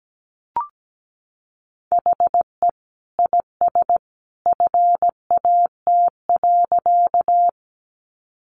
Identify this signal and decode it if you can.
Morse code, heard as a keyed tone, about 17 wpm, 700 Hz. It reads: HE IS FAT.
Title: Sentences from Top 600 Words 17wpm